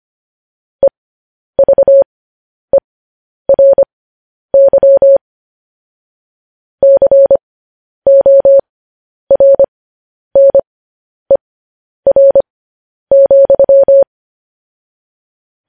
Morse code, heard as a keyed tone, about 25 wpm, spaced as for 10 wpm, 550 Hz. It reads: EVERY CORNER,
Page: News Headlines in Morse Code at 10 WPM